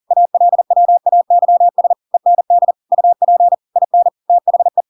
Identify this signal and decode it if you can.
Morse code, heard as a keyed tone, about 40 wpm, 700 Hz. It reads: ALWAYS END UP IN THE
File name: Pinocchio / Pinocchio ch4 40 40 ik1hge